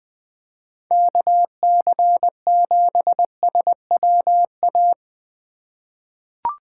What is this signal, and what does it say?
Morse code, heard as a keyed tone, about 21 wpm, 700 Hz. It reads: KC7SWA
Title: USA Call Signs - 6 Characters - 4x Character Spacing - 20wpm